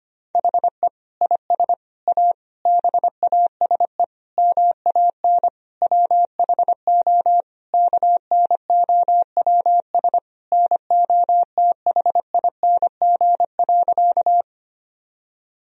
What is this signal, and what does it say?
Morse code, heard as a keyed tone, about 25 wpm, 700 Hz. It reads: HE IS A BASE MAN WHO KNOWS NOTHING.
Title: Sentences from Top 1000 Words - Review Part 1 - 25wpm